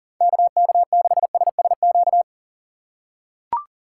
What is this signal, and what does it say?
Morse code, heard as a keyed tone, about 37 wpm, 700 Hz. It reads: KK6SSQ
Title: USA Call Signs - 4x Character Spacing - 40wpm